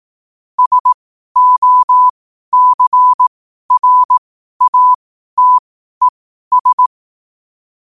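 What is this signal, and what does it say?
Morse code, heard as a keyed tone, about 18 wpm, spaced as for 12 wpm, 1000 Hz. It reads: SOCRATES
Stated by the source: Morse code sent at 12 words per minute.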